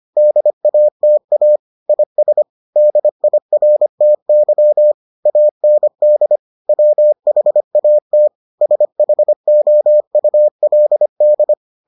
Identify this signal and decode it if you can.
Morse code, heard as a keyed tone, about 25 wpm, 600 Hz. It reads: DATA IS DIRTY AND WHAT SHOULD